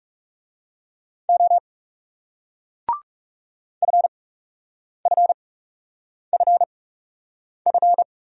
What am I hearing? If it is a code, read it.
Morse code, heard as a keyed tone, about 40 wpm, spaced as for 8 wpm, 700 Hz. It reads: O FFFF